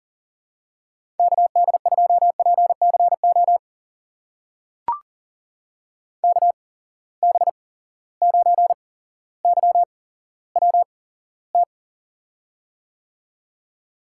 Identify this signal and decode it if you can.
Morse code, heard as a keyed tone, about 40 wpm, spaced as for 17 wpm, 700 Hz. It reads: KD2PCO KB9YWT
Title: USA Call Signs - 6 Characters - 8x Character Spacing - 40wpm